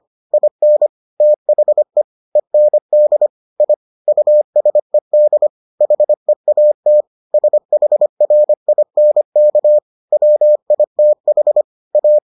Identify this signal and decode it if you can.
Morse code, heard as a keyed tone, about 25 wpm, 600 Hz. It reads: IN THE END I USED HEAT SHRINK WITH A